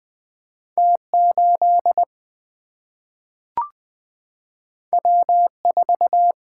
Morse code, heard as a keyed tone, about 20 wpm, 700 Hz. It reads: T 8 W 4
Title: Single-Letter Single-Number 20wpm